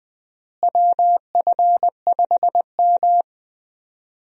Morse code, heard as a keyed tone, about 20 wpm, 700 Hz. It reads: WF5M
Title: USA Call Signs - 4 Characters - Repeated 3x - 20wpm